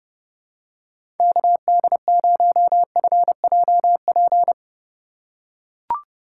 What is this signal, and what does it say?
Morse code, heard as a keyed tone, about 30 wpm, 700 Hz. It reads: KD0FJP
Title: USA Call Signs - 6 Characters - 4x Character Spacing - 30wpm